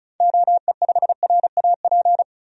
Morse code, heard as a keyed tone, about 34 wpm, 700 Hz. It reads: OE5RAP
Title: DXCC-100 Call Signs - 6 Characters - 35wpm